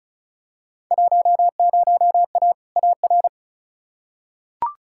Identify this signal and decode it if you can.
Morse code, heard as a keyed tone, about 34 wpm, 700 Hz. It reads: 10A AR
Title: Field Day Exchanges Encore - 35wpm